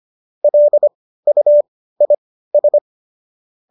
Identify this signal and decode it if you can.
Morse code, heard as a keyed tone, about 25 wpm, spaced as for 15 wpm, 570 Hz. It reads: LUIS
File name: qso204 15